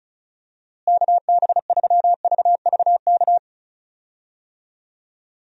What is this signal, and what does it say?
Morse code, heard as a keyed tone, about 34 wpm, 700 Hz. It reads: KB3VVK
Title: USA Call Signs - 6 Characters - Repeated 3x - 35wpm